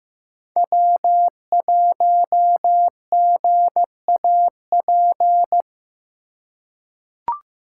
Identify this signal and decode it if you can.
Morse code, heard as a keyed tone, about 15 wpm, 700 Hz. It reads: W1GAP
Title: USA Call Signs - 5 Characters - 2x Character Spacing - 15wpm